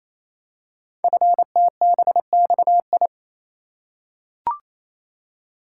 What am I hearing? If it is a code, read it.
Morse code, heard as a keyed tone, about 27 wpm, 700 Hz. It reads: FTBXI
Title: Code Groups - 5 Letters - 28wpm